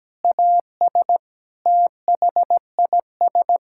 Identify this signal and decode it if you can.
Morse code, heard as a keyed tone, about 17 wpm, 700 Hz. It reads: AS THIS
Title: Sentences from Top 1600 Words 17wpm